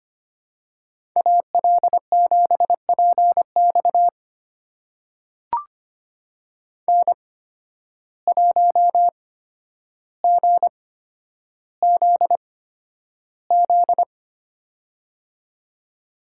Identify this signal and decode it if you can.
Morse code, heard as a keyed tone, about 25 wpm, spaced as for 11 wpm, 700 Hz. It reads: AL7PX N1GZZ
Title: USA Call Signs - 5 Characters - 8x Character Spacing - 25wpm